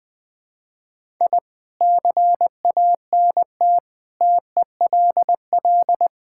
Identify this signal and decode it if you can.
Morse code, heard as a keyed tone, about 21 wpm, 700 Hz. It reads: I CANT TELL
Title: Sets of 8 Words - 20wpm